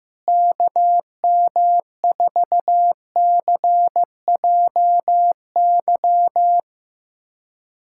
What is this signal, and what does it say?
Morse code, heard as a keyed tone, about 16 wpm, 700 Hz. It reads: KM4CJY
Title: USA Callsign DE Callsign - 6 Characters - 15wpm